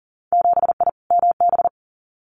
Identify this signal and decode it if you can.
Morse code, heard as a keyed tone, about 42 wpm, 700 Hz. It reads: 7I MB
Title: Winter Field Day Exchanges Encore 40wpm